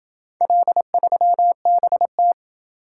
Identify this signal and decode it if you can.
Morse code, heard as a keyed tone, about 26 wpm, 700 Hz. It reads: L3BT